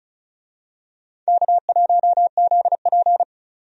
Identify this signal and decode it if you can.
Morse code, heard as a keyed tone, about 34 wpm, 700 Hz. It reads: K1ZP
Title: USA Call Signs - 4 Characters - 35wpm